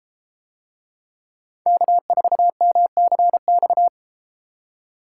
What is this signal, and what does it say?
Morse code, heard as a keyed tone, about 32 wpm, 700 Hz. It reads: K4MCX